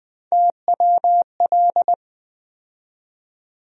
Morse code, heard as a keyed tone, about 21 wpm, 700 Hz. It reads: TWL
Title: USA Call Sign Postfixes - 3 Letters 20wpm